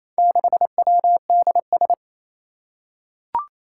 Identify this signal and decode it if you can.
Morse code, heard as a keyed tone, about 27 wpm, 700 Hz. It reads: 6WDS
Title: Four Challenging Letter-Number Combinations 28wpm